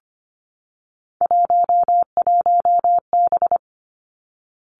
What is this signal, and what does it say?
Morse code, heard as a keyed tone, about 25 wpm, 700 Hz. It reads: 11B
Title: Three Challenging Letter-Number Combinations 25wpm